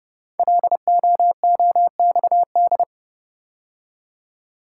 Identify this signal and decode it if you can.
Morse code, heard as a keyed tone, about 30 wpm, 700 Hz. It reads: LOOXD
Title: Code Groups - 5 Letter:Numbers - 30wpm